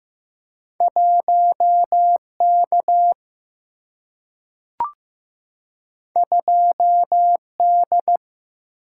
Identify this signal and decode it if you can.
Morse code, heard as a keyed tone, about 15 wpm, 700 Hz. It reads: 1K 2D
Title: Single-Letter Single-Number 15wpm